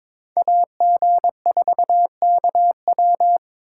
Morse code, heard as a keyed tone, about 26 wpm, 700 Hz. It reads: AG4KW
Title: Speed Racing - USA Call Signs 22wpm